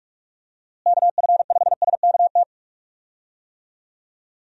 Morse code, heard as a keyed tone, about 46 wpm, 700 Hz. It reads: KF5SKT
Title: USA Call Signs - 6 Characters - Repeated 2x - 45wpm